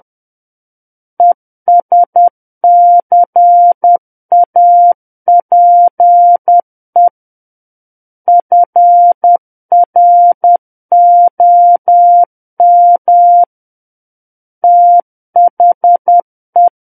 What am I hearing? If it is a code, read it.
Morse code, heard as a keyed tone, about 10 wpm, 700 Hz. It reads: ESCAPE FROM THE
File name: Book One / 01-Chapter 1 - 10 WPM